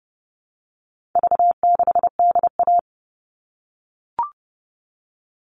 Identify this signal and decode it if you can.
Morse code, heard as a keyed tone, about 30 wpm, 700 Hz. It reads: V6DA
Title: All Cut Numbers - 4-Digit 30wpm